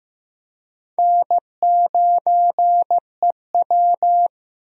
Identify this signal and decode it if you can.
Morse code, heard as a keyed tone, about 15 wpm, 700 Hz. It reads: N9EW
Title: USA Call Signs - 4 Characters - 15wpm